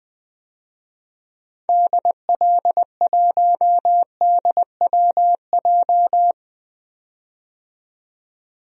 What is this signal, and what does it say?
Morse code, heard as a keyed tone, about 20 wpm, 700 Hz. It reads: DL1DWJ
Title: DXCC-100 Call Signs - 6 Characters - 20wpm